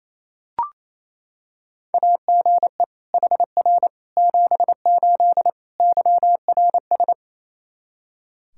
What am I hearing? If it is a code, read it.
Morse code, heard as a keyed tone, about 28 wpm, 700 Hz. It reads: AGE HR 78 YRS
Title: Practice QSO - Exchange 3 - More Info - 28wpm